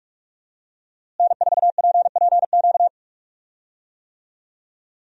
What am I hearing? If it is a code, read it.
Morse code, heard as a keyed tone, about 47 wpm, 700 Hz. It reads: N4PPQ
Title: USA Call Signs - 5 Characters - Repeated 3x - 45wpm